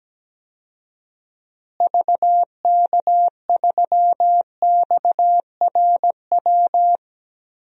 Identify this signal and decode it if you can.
Morse code, heard as a keyed tone, about 17 wpm, 700 Hz. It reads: VK3XRW
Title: DXCC-100 Call Signs - 6 Characters - Repeated 2x - 17wpm